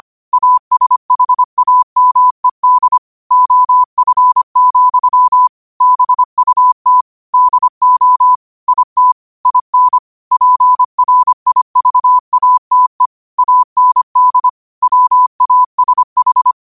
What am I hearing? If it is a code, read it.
Morse code, heard as a keyed tone, about 25 wpm, 1000 Hz. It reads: ASHAMED OF, BUT DO IT IN PRIVATE AND WASH
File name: qotdfiles / 25wpm